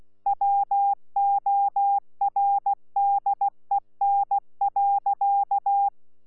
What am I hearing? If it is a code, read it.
Morse code, heard as a keyed tone, about 16 wpm, 800 Hz. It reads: WORDEN.
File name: Z PI4VRZ / PI4VRZ 14-jun-2025